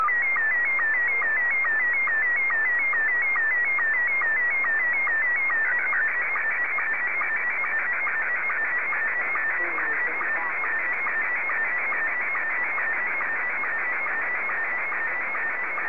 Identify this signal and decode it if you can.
SSTV (slow-scan television), Scottie 1.